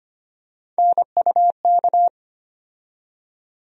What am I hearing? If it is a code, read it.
Morse code, heard as a keyed tone, about 25 wpm, 700 Hz. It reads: NUK
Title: USA Call Sign Postfixes - 3 Letters 25wpm